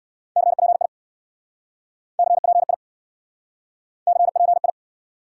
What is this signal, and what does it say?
Morse code, heard as a keyed tone, about 67 wpm, 700 Hz. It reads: CPI CPI CPI